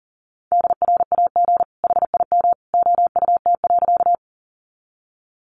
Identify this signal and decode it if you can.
Morse code, heard as a keyed tone, about 40 wpm, 700 Hz. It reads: DRAG HIM OUT.
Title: Sentences from Top 1500 Words 40wpm